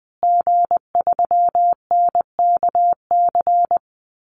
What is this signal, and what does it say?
Morse code, heard as a keyed tone, about 19 wpm, 700 Hz. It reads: G3NKC
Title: Call Signs Repeated 2x - 20wpm